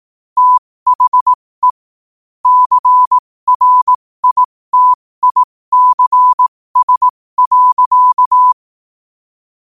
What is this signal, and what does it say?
Morse code, heard as a keyed tone, about 18 wpm, spaced as for 15 wpm, 1000 Hz. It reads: THE CRITICS.
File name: qotdfiles / 15wpm